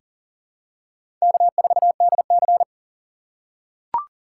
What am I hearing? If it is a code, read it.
Morse code, heard as a keyed tone, about 40 wpm, 700 Hz. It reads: K4DC
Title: USA Call Signs - 4 Characters - 3x Character Spacing - 40wpm